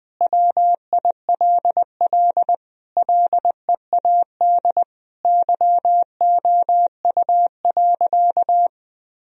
Morse code, heard as a keyed tone, about 20 wpm, 700 Hz. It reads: WILL LEAD YOU.
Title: Sentences from Top 1000 Words - Review Part 2 - 20wpm